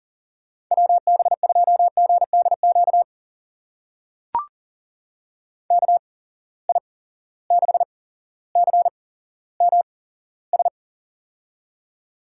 Morse code, heard as a keyed tone, about 40 wpm, spaced as for 17 wpm, 700 Hz. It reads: WB2GDQ KI6CMS
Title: USA Call Signs - 6 Characters - 8x Character Spacing - 40wpm